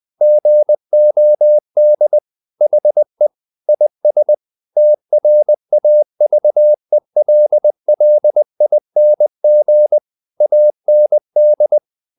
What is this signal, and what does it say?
Morse code, heard as a keyed tone, about 20 wpm, 600 Hz. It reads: GOD HE IS TRAVELLING AND
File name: the old man and the sea / the old man and the sea part 1 20wpm